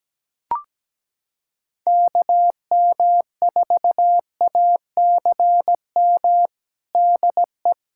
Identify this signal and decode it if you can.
Morse code, heard as a keyed tone, about 17 wpm, 700 Hz. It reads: KM4ACM DE
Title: USA Callsign DE Callsign 17wpm